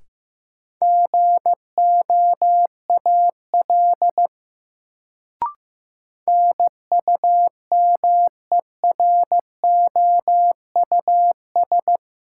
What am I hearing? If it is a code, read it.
Morse code, heard as a keyed tone, about 15 wpm, 700 Hz. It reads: GOAL NUMEROUS
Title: Sentences from Top 1400 Words - New Word Drill Alt - 15wpm